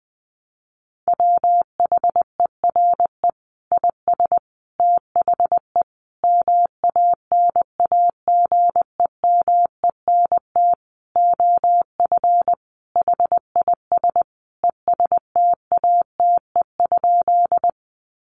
Morse code, heard as a keyed tone, about 20 wpm, 700 Hz. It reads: WHERE IS THE MANAGEMENT OF HIS ESTATE?
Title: Sentences from Top 1200 Words 20wpm